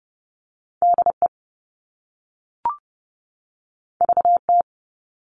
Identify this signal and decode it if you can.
Morse code, heard as a keyed tone, about 30 wpm, 700 Hz. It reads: DE VT
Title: US States 30wpm